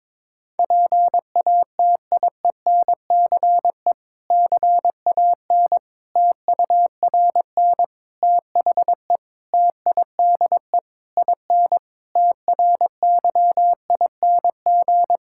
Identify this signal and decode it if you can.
Morse code, heard as a keyed tone, about 22 wpm, 700 Hz. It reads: PATIENCE CAN TURN THE TIDE IN TRYING
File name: Difficult Sentences with Hints - 22wpm